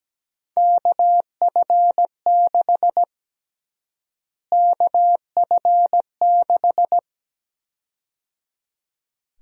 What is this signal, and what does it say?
Morse code, heard as a keyed tone, about 17 wpm, 700 Hz. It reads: KF6 KF6
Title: USA Call Sign Prefixes - Repeated 2x - 17wpm